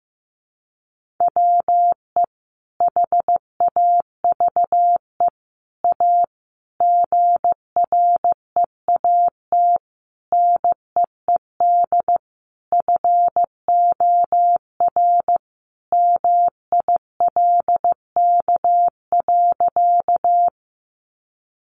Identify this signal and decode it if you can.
Morse code, heard as a keyed tone, about 15 wpm, 700 Hz. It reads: WE HAVE A GREAT NEED FOR MILK.
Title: Sentences from Top 1500 Words 15wpm